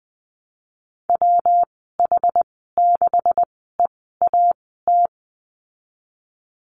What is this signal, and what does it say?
Morse code, heard as a keyed tone, about 20 wpm, spaced as for 14 wpm, 700 Hz. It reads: WH6EAT
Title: 193 Lesson 31g - Letter v - Call Signs 20wpm